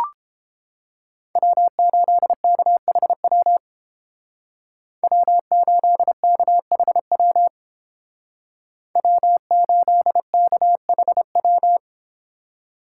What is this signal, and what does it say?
Morse code, heard as a keyed tone, about 28 wpm, 700 Hz. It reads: W8KHW W8KHW W8KHW